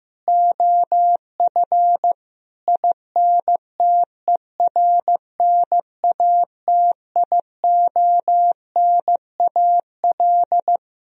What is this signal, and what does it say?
Morse code, heard as a keyed tone, about 15 wpm, 700 Hz. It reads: OF INTERNATIONAL
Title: Sentences from Top 1000 Words - Review Part 4 - 15wpm